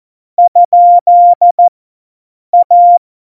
Morse code, heard as a keyed tone, about 14 wpm, 700 Hz. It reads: ? A
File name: Morsekurs / Witze